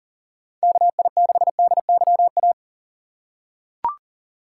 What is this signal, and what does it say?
Morse code, heard as a keyed tone, about 40 wpm, 700 Hz. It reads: KI6DYA
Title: USA Call Signs - 6 Characters - 2x Character Spacing - 40wpm